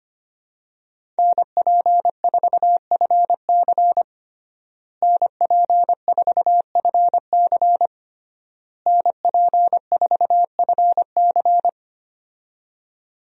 Morse code, heard as a keyed tone, about 25 wpm, 700 Hz. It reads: NP4FC NP4FC NP4FC
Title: USA Call Signs - 5 Characters - Repeated 3x - 25wpm